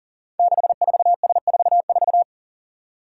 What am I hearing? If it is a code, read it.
Morse code, heard as a keyed tone, about 38 wpm, 690 Hz. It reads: 64S44
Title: Five Challenging Letter-Number Combinations 40wpm